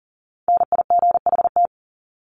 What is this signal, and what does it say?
Morse code, heard as a keyed tone, about 40 wpm, 700 Hz. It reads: NIGHT
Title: Top 500 Words 40wpm